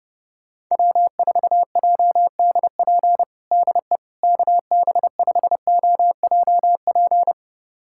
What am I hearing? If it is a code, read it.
Morse code, heard as a keyed tone, about 32 wpm, 700 Hz. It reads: W4JDP DE KB5OJP
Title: USA Callsign DE Callsign 30wpm